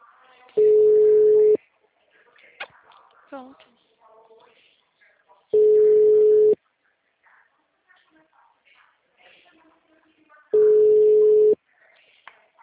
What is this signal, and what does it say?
Ringback tone (Europe, 425 Hz), with speech.
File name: out- -2012- - - .141